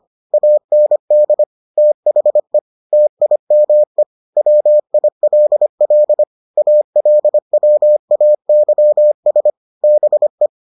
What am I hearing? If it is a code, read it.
Morse code, heard as a keyed tone, about 25 wpm, 600 Hz. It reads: AND THE TIME WILL ALWAYS BE